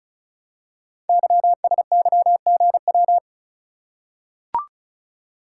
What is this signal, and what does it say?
Morse code, heard as a keyed tone, about 36 wpm, 700 Hz. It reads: YSYGW
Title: Code Groups - 5 Letters - 35wpm